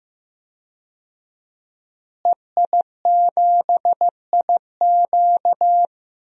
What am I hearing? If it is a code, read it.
Morse code, heard as a keyed tone, about 16 wpm, 700 Hz. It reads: EI7IQ
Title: DXCC-100 Call Signs - 5 Characters - Repeated 3x - 15wpm